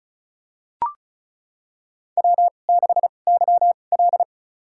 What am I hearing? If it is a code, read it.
Morse code, heard as a keyed tone, about 36 wpm, spaced as for 25 wpm, 700 Hz. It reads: W6YL
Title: USA Call Signs - 2x Character Spacing - 35wpm